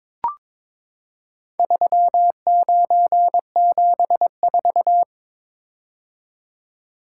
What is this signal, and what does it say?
Morse code, heard as a keyed tone, about 22 wpm, 700 Hz. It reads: 3974
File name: practice / 22wpm / Four Letter-Number Combinations - 22wpm